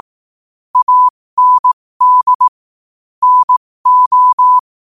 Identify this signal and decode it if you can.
Morse code, heard as a keyed tone, about 18 wpm, spaced as for 15 wpm, 1000 Hz. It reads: AND NO
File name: qotdfiles / 15wpm